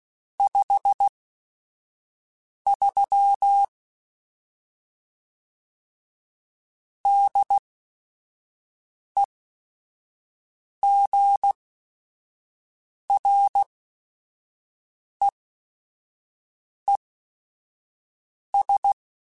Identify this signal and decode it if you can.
Morse code, heard as a keyed tone, about 16 wpm, spaced as for 5 wpm, 800 Hz. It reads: 53 DEGREES
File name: Weather Sets / Weather Sets-mp3s / Weather 5 1 @5